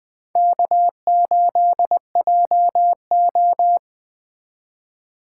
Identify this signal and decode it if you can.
Morse code, heard as a keyed tone, about 20 wpm, 700 Hz. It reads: K8JO
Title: Speed Racing - USA 1x2 Call Signs 17wpm